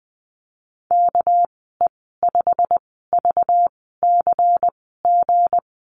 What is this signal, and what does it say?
Morse code, heard as a keyed tone, about 19 wpm, spaced as for 14 wpm, 700 Hz. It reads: KE5VCG
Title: 200 Lesson 32g - Letter k - Call Signs 20wpm